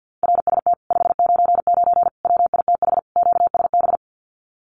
Morse code, heard as a keyed tone, about 48 wpm, 700 Hz. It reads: RST 599 WITH QSB
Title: Common QSO Phrases Encore 50wpm